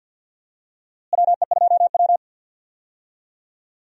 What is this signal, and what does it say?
Morse code, heard as a keyed tone, about 52 wpm, 700 Hz. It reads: WE2W
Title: USA Call Signs - 4 Characters - Repeated 2x - 50wpm